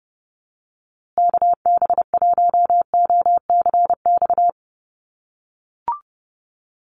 Morse code, heard as a keyed tone, about 30 wpm, 700 Hz. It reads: KB1OCX
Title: USA Call Signs Repeated 3x 30wpm